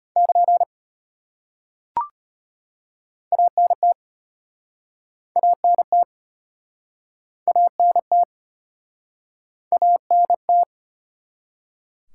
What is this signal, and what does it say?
Morse code, heard as a keyed tone, about 33 wpm, 700 Hz. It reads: <KN> ANT ANT ANT ANT